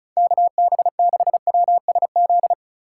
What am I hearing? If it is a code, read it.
Morse code, heard as a keyed tone, about 36 wpm, 700 Hz. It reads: KB6WSZ
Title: USA Call Signs - 6 Characters - 8x Character Spacing - 35wpm